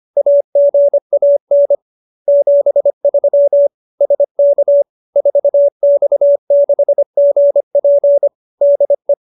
Morse code, heard as a keyed tone, about 25 wpm, 570 Hz. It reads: AGAN 73 SK 4X6GP DE